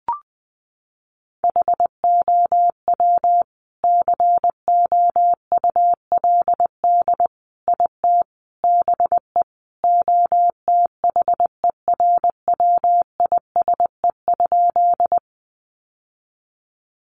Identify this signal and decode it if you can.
Morse code, heard as a keyed tone, about 20 wpm, 700 Hz. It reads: HOW COULD IT BE OTHERWISE?
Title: Sentences from Top 1200 Words 20wpm